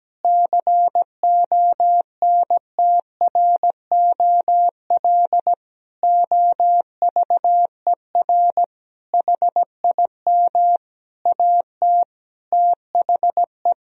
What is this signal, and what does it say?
Morse code, heard as a keyed tone, about 17 wpm, 700 Hz. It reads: CONTROL OVER HIM AT THE
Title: Sentences from Top 1000 Words - Review Part 3 - 17wpm